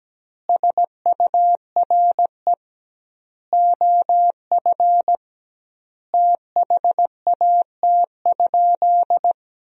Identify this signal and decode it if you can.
Morse code, heard as a keyed tone, about 17 wpm, 700 Hz. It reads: SURE OF THAT?
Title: 2x Word-Spacing - Sentences from Top 300 Words - 17wpm